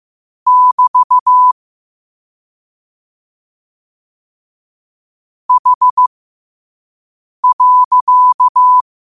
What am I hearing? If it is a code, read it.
Morse code, heard as a keyed tone, about 15 wpm, spaced as for 5 wpm, 1000 Hz. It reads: <BT> H.
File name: qotdfiles / 5wpm